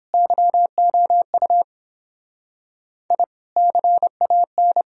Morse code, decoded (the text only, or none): YOU I CAN